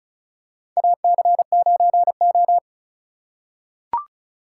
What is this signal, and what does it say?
Morse code, heard as a keyed tone, about 34 wpm, 700 Hz. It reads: AC9O
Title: USA Call Signs - 4 Characters - 2x Character Spacing - 35wpm